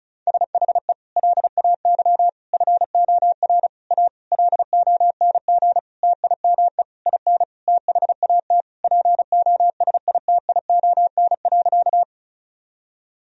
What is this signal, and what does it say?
Morse code, heard as a keyed tone, about 36 wpm, 700 Hz. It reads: SHE LAY FOR A LONG TIME IN THAT POSITION.
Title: Sentences from Top 1000 Words - Review Part 2 - 35wpm